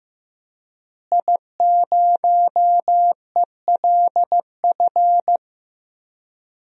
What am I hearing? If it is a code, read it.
Morse code, heard as a keyed tone, about 15 wpm, 700 Hz. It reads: I0ELF
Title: DXCC-100 Call Signs - 5 Characters - Repeated 2x - 15wpm